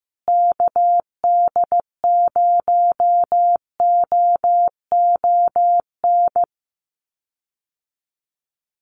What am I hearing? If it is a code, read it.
Morse code, heard as a keyed tone, about 15 wpm, 700 Hz. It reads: KD0OON
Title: USA Call Signs Repeated 2x 15wpm